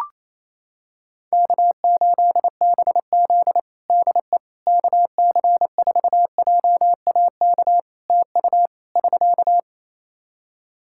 Morse code, heard as a keyed tone, about 28 wpm, 700 Hz. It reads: K8BZ DE KC4JAK TU <SK>
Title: Practice QSO - Exchange 4 - Ending - 28wpm